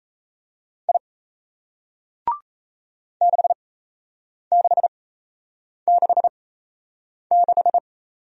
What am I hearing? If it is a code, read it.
Morse code, heard as a keyed tone, about 36 wpm, spaced as for 8 wpm, 700 Hz. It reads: I 6666